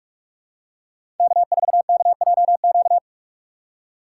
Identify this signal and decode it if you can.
Morse code, heard as a keyed tone, about 46 wpm, 700 Hz. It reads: K4KJQ
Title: USA Call Signs - 5 Characters - 4x Character Spacing - 45wpm